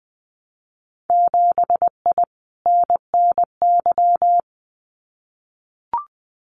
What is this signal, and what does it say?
Morse code, heard as a keyed tone, about 20 wpm, 700 Hz. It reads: 7I NNY
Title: Winter Field Day Exchanges Encore 20wpm